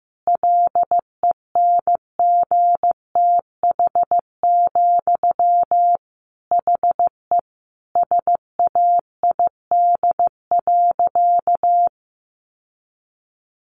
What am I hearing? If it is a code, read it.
Morse code, heard as a keyed tone, about 14 wpm, 700 Hz. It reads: LENGTH, HE SAID.
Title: Sentences from Top 1000 Words 15wpm